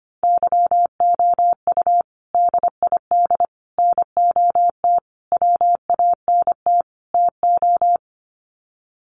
Morse code, heard as a keyed tone, about 25 wpm, 700 Hz. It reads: YOU DID NOT WANT TO
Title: Sets of 5 Words Encore2 25wpm